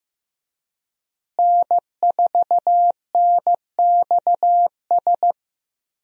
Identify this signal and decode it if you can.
Morse code, heard as a keyed tone, about 15 wpm, 700 Hz. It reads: N4NXS
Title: USA Call Signs - 5 Characters - Repeated 2x - 15wpm